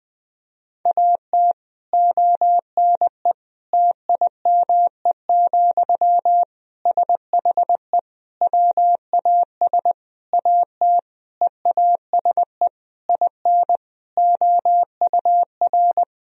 Morse code, heard as a keyed tone, about 20 wpm, 700 Hz. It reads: AT ONE TIME, SHE WAS AT EASE IN OUR
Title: Sentences from Top 1000 Words - Review Part 1 - 20wpm